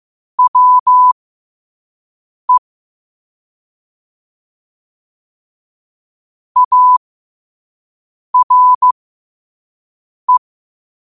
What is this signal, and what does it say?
Morse code, heard as a keyed tone, about 15 wpm, spaced as for 5 wpm, 1000 Hz. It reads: WE ARE
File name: 5wpm